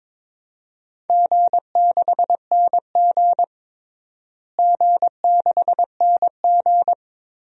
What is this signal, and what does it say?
Morse code, heard as a keyed tone, about 21 wpm, 700 Hz. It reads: G6NG G6NG
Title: DXCC-100 Call Signs - 4 Characters - Repeated 2x - 22wpm